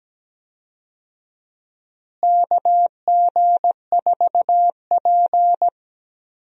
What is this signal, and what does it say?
Morse code, heard as a keyed tone, about 17 wpm, 700 Hz. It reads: KG4P
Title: USA Call Signs - 4 Characters - Repeated 3x - 17wpm